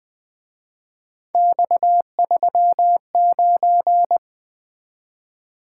Morse code, heard as a keyed tone, about 25 wpm, 700 Hz. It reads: X39